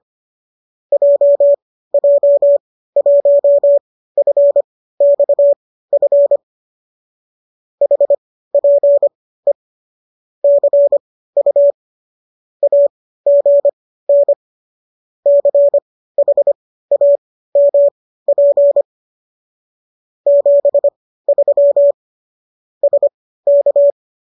Morse code, heard as a keyed tone, about 25 wpm, spaced as for 15 wpm, 570 Hz. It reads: JJ1FXF HPE CU AGN CHAMP 73 SK